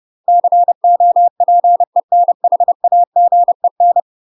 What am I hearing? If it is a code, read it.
Morse code, heard as a keyed tone, about 30 wpm, 700 Hz. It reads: COPENHAGEN
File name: UK MASTER QSOs TRAINING / 30WPM / UK MASTER QSOs 0046